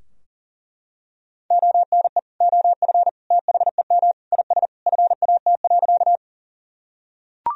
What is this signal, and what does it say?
Morse code, heard as a keyed tone, about 40 wpm, 700 Hz. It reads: ONE OF THEM IS FAT.
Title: Sentences from Top 1000 Words - Review Part 4 - 40wpm